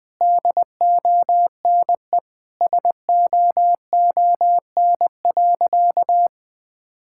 Morse code, heard as a keyed tone, about 21 wpm, 700 Hz. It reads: DONE SOON.
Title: Sentences from Top 1600 Words 20wpm